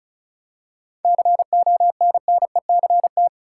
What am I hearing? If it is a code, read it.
Morse code, heard as a keyed tone, about 36 wpm, 700 Hz. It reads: CONNECT
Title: Seven Letter Words - 35wpm